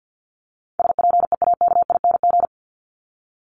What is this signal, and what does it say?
Morse code, heard as a keyed tone, about 49 wpm, 700 Hz. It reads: SPEAKING
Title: ING Suffix - Top 100 Words with Base Words 50wpm